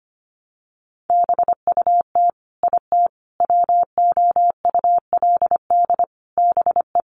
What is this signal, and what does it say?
Morse code, heard as a keyed tone, about 25 wpm, 700 Hz. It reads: BUT IT WOULD BE
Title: Sentences from Top 600 Words 25wpm